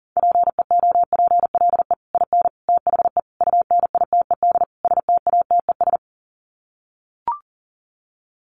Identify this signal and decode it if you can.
Morse code, heard as a keyed tone, about 40 wpm, 700 Hz. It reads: PEOPLE IN THE UNITED STATES
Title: Sets of 5 Words Encore 40wpm